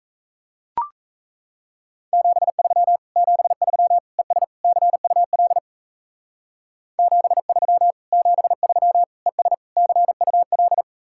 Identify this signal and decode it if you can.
Morse code, heard as a keyed tone, about 34 wpm, 700 Hz. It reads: 73 73 ES CUL 73 73 ES CUL
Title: Speed-Racing Common QSO Phrases Encore - 28wpm